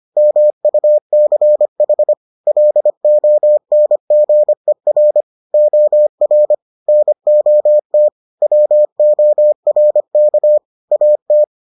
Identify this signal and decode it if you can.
Morse code, heard as a keyed tone, about 25 wpm, 600 Hz. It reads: MUCH LONGER OR NOT WORK AT